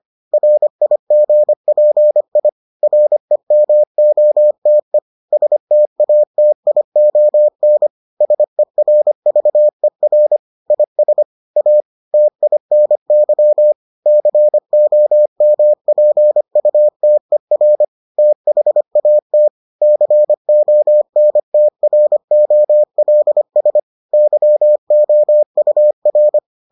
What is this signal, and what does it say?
Morse code, heard as a keyed tone, about 25 wpm, 600 Hz. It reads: RIGPI REMOTE STATION SERVER IS A TINY COMPUTER THAT CONTROLS YOUR